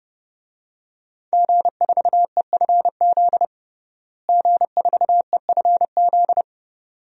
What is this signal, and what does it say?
Morse code, heard as a keyed tone, about 30 wpm, 700 Hz. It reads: G4EFZ G4EFZ
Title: DXCC-100 Call Signs - 5 Characters - Repeated 2x - 30wpm